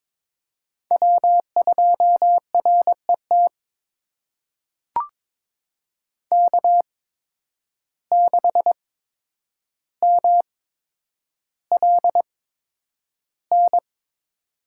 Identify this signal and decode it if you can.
Morse code, heard as a keyed tone, about 22 wpm, spaced as for 9 wpm, 700 Hz. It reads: W2RET K6MLN
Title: USA Call Signs - 8x Character Spacing - 22wpm